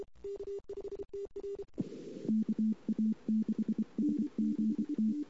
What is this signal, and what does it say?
Morse code, heard as a keyed tone, about 24 wpm, 220 Hz. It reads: E KA6SGT
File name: CW / 2008-ss-cw-k5tr-op-n5rz -22:00